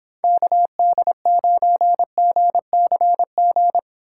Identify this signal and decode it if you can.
Morse code, heard as a keyed tone, about 31 wpm, 700 Hz. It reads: KD9GCG